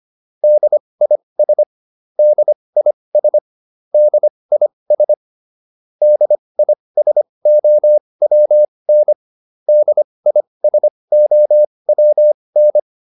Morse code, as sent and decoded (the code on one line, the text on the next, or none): -.. .. ... / -.. .. ... / -.. .. ... / -.. .. ... --- .-- -. / -.. .. ... --- .-- -.
DIS DIS DIS DISOWN DISOWN